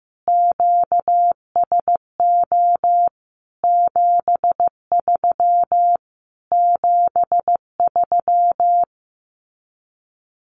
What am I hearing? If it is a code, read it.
Morse code, heard as a keyed tone, about 15 wpm, 700 Hz. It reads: QSO 73 73
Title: Common QSO Phrases Encore 15wpm